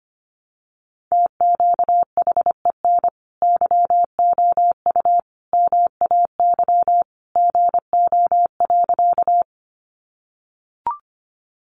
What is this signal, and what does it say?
Morse code, heard as a keyed tone, about 25 wpm, 700 Hz. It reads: TQHEN YOU MAY GO.
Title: Sentences from Top 100 Words 25wpm